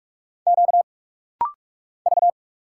Morse code, heard as a keyed tone, about 44 wpm, 700 Hz. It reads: Q V
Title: Single Letter-Number Alt - Rapid-Fire 45wpm